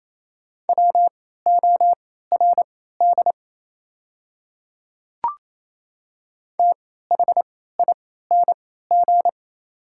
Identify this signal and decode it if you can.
Morse code, heard as a keyed tone, about 27 wpm, spaced as for 16 wpm, 690 Hz. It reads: WORD THING
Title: 3x Character Spacing - Top 100 Words - 28wpm